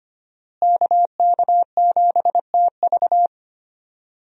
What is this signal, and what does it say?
Morse code, heard as a keyed tone, about 25 wpm, 700 Hz. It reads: KK7TV
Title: USA Call Signs - 5 Characters - Repeated 3x - 25wpm